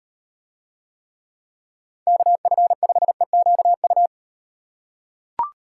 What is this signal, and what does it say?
Morse code, heard as a keyed tone, about 35 wpm, 700 Hz. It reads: KF5EQU